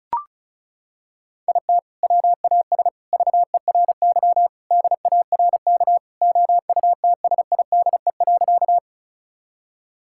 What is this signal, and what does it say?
Morse code, heard as a keyed tone, about 36 wpm, 700 Hz. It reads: IT WAS VERY DARK OUTSIDE.
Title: Sentences from Top 1000 Words - Review Part 6 - 35wpm